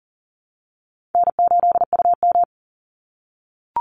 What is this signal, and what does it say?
Morse code, heard as a keyed tone, about 40 wpm, 700 Hz. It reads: N8UM
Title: Call Signs - 40wpm